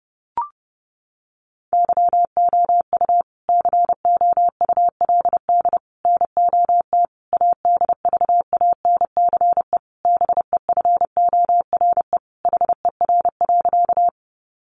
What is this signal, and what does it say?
Morse code, heard as a keyed tone, about 30 wpm, 700 Hz. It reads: YOU COULD NOT ADVANCE BEFORE HER.
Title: Sentences from Top 1400 Words 30wpm